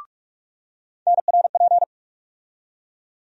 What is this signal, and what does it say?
Morse code, heard as a keyed tone, about 46 wpm, 700 Hz. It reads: NRP
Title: USA Call Sign Postfixes - 3 Letters 45wpm